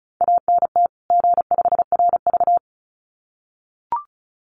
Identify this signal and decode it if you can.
Morse code, heard as a keyed tone, about 34 wpm, 700 Hz. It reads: ANT G5RV
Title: Common QSO Phrases 35wpm